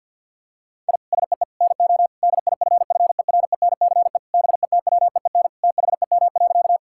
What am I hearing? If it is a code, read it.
Morse code, heard as a keyed tone, about 50 wpm, 700 Hz. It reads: I SEE NO DIFFERENCE BETWEEN THEM.
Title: Sentences from Top 1000 Words - Review Part 5 - 50wpm